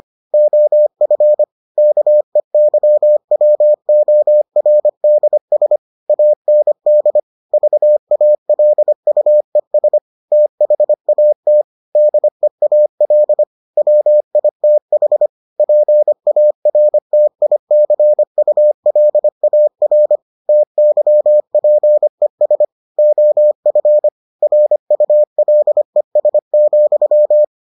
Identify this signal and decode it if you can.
Morse code, heard as a keyed tone, about 25 wpm, 600 Hz. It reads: OF KEYWORDS AND VALUES THAT DEAL WITH PARTICULAR TYPES OF RULES,